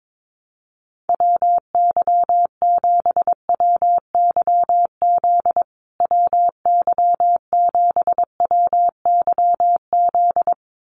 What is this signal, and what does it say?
Morse code, heard as a keyed tone, about 22 wpm, 700 Hz. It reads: WY7WYZ WY7WYZ
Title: USA Call Signs Repeated 2x 22wpm